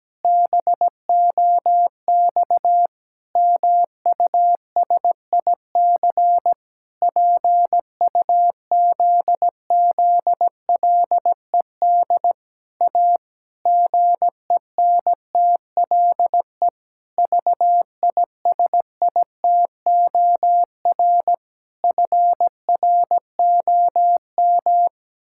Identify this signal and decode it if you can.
Morse code, heard as a keyed tone, about 17 wpm, 700 Hz. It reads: BOX MUSIC PUZZLED A GENTLE VISITOR FROM
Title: Pangram Challenge v2 - 17wpm